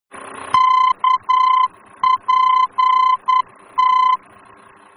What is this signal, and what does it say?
Morse code, heard as a keyed tone, about 10 wpm, 1020 Hz. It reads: KPT
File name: germany / mx / KPT-D-VOR